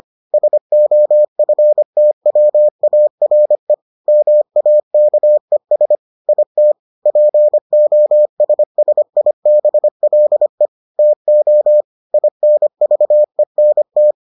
Morse code, heard as a keyed tone, about 25 wpm, 600 Hz. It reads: SOFTWARE MAKES IT POSSIBLE TO INVENT